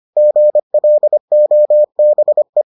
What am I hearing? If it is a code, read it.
Morse code, heard as a keyed tone, about 25 wpm, 600 Hz. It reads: GLOBE